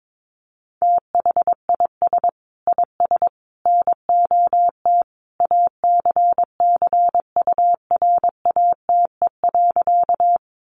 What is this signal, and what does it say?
Morse code, heard as a keyed tone, about 22 wpm, 700 Hz. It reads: THIS IS NOT ACCURATE.
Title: Sentences from Top 1200 Words 22wpm